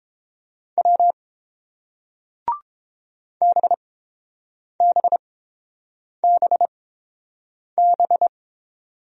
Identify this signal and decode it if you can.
Morse code, heard as a keyed tone, about 29 wpm, spaced as for 7 wpm, 700 Hz. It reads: W BBBB